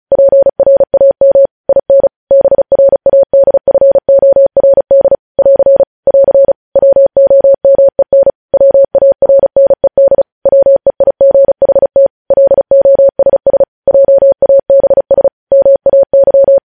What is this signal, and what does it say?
Morse code, heard as a keyed tone, about 36 wpm, 550 Hz. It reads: PRAM IN BRADFORD <AR> <AR> WOMEN WARNED WEIGHT LOSS JABS MAY
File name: podcast / News headlines-1x-35x35